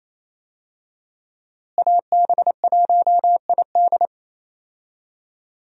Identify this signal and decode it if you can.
Morse code, heard as a keyed tone, about 27 wpm, 700 Hz. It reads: AB1ID
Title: USA Call Signs - 5 Characters - Repeated 3x - 28wpm